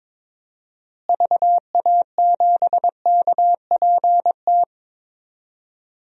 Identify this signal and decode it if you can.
Morse code, heard as a keyed tone, about 22 wpm, 700 Hz. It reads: VA7KPT
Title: DXCC-100 Call Signs - 6 Characters - Repeated 2x - 22wpm